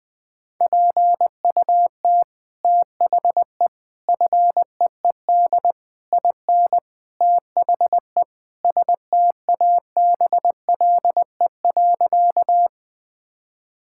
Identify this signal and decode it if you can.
Morse code, heard as a keyed tone, about 20 wpm, 700 Hz. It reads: PUT THE FEED IN THE STABLE.
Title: Sentences from Top 1600 Words 20wpm